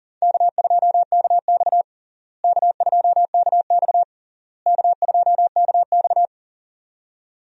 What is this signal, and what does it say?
Morse code, heard as a keyed tone, about 40 wpm, 700 Hz. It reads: K2KX K2KX K2KX
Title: USA Call Signs - 4 Characters - Repeated 3x - 40wpm